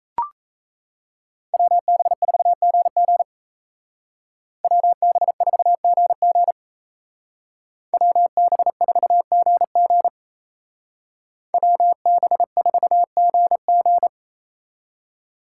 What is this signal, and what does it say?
Morse code, heard as a keyed tone, about 36 wpm, 700 Hz. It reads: WB4GG WB4GG WB4GG WB4GG